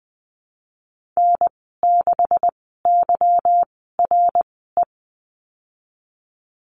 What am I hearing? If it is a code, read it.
Morse code, heard as a keyed tone, about 20 wpm, spaced as for 15 wpm, 700 Hz. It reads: N6YRE
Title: 154 Lesson 25g - Letter p - Call Signs 20wpm